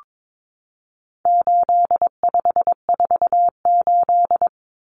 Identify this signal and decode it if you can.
Morse code, heard as a keyed tone, about 22 wpm, 700 Hz. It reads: 8548
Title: Numbers - 4-Digit 22wpm